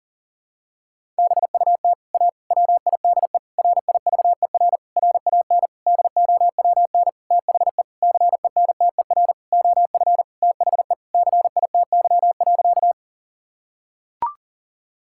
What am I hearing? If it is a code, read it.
Morse code, heard as a keyed tone, about 40 wpm, 700 Hz. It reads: BUT A WIDE RIVER RAN DOWN THE CENTER OF THE CITY.
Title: Sentences from Top 1000 Words - Review Part 6 - 40wpm